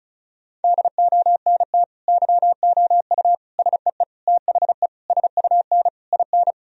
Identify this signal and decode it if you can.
Morse code, heard as a keyed tone, about 34 wpm, 700 Hz. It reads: DONT YOU SEE THE SUN IN
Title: Sentences from Top 1000 Words - Review Part 2 - 35wpm